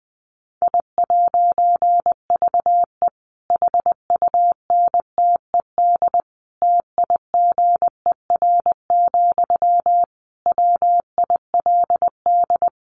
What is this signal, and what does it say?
Morse code, heard as a keyed tone, about 20 wpm, 700 Hz. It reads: I'VE HUNTED TIGER, WILD
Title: A Sound Of Thunder 20wpm